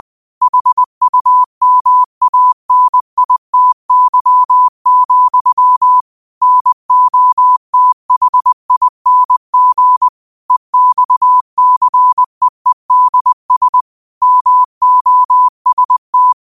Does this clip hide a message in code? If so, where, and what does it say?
Yes: Morse code from 0.4 to 16.3 s, heard as a keyed tone, about 20 wpm, 1000 Hz. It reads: HUMANITY, NOTHING EXCEEDS MOST